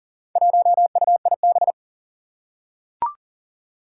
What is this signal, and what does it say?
Morse code, heard as a keyed tone, about 41 wpm, 700 Hz. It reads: 1UIB
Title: Four Challenging Letter-Number Combinations 40wpm